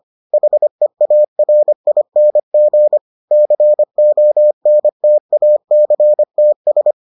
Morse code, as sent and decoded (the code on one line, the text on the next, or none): .... .- .-. .. -. --. / -.-. --- -. - .- -.-. - ...
HEARING CONTACTS